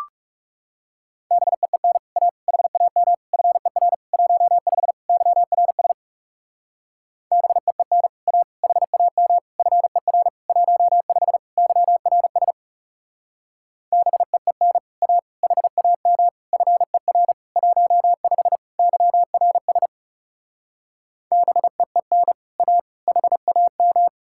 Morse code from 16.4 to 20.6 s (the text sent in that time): FER 15 YRS